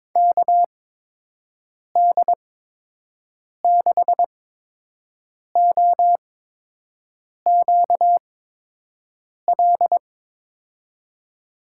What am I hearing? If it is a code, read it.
Morse code, heard as a keyed tone, about 22 wpm, spaced as for 6 wpm, 700 Hz. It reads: KD6OQL
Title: USA Call Signs - 6 Characters - 8x Character Spacing - 22wpm